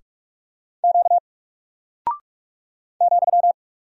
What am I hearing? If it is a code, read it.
Morse code, heard as a keyed tone, about 44 wpm, 700 Hz. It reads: Q ,
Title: Single Character (Letter:Number:Punctuation:Prosign) 45wpm